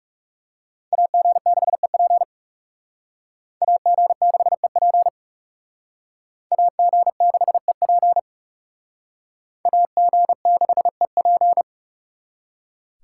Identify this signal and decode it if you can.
Morse code, heard as a keyed tone, about 38 wpm, 700 Hz. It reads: AG6EP AG6EP AG6EP AG6EP